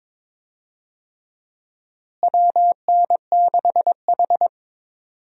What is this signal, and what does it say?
Morse code, heard as a keyed tone, about 22 wpm, 700 Hz. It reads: WN6H
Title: USA Call Signs - 4 Characters - Repeated 3x - 22wpm